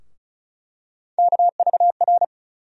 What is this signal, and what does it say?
Morse code, heard as a keyed tone, about 34 wpm, 700 Hz. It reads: KVR
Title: USA Call Sign Postfixes - 3 Letters 35wpm